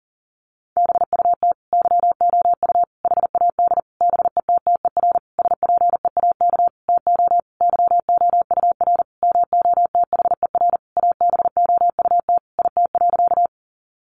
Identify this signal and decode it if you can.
Morse code, heard as a keyed tone, about 40 wpm, 700 Hz. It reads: BUT YOU HAD BETTER SPEAK TO YOUR MOTHER ABOUT IT.
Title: Sentences from Top 1100 Words 40wpm